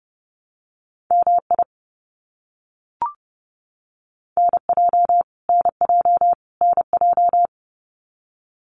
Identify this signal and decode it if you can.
Morse code, heard as a keyed tone, about 30 wpm, 700 Hz. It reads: MI NJ NJ NJ
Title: US States Repeated 3x 30wpm